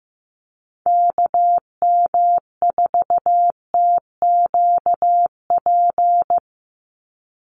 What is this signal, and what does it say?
Morse code, heard as a keyed tone, about 15 wpm, 700 Hz. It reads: KM4TQP
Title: USA Call Signs 15wpm